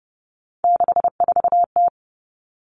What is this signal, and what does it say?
Morse code, heard as a keyed tone, about 30 wpm, 700 Hz. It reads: 64T
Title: Common Cut Numbers - 3-Digit 30wpm